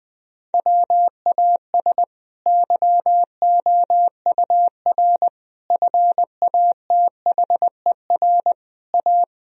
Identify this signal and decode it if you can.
Morse code, heard as a keyed tone, about 20 wpm, 700 Hz. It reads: WAS YOUR FATHER A
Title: Sentences from Top 1000 Words - Review Part 4 - 20wpm